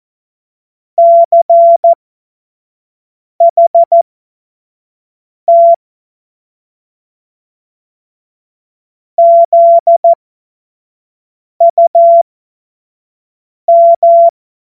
Morse code, heard as a keyed tone, about 14 wpm, spaced as for 5 wpm, 680 Hz. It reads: CHT ZUM
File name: Morsekurs / Witze / Witz 05. Oktober 2021, 25 F70